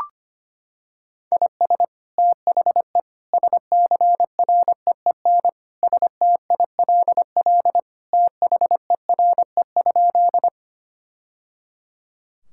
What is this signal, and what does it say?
Morse code, heard as a keyed tone, about 25 wpm, 700 Hz. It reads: IS THE SCREEN STILL THERE?
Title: Sentences from Top 1000 Words - Review Part 6 - 25wpm